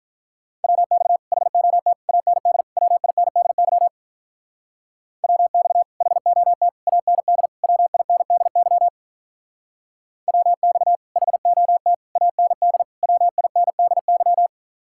Morse code, heard as a keyed tone, about 43 wpm, 700 Hz. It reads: WX HOT AND WINDY WX HOT AND WINDY WX HOT AND WINDY
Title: Speed-Racing Common QSO Phrases Encore - 35wpm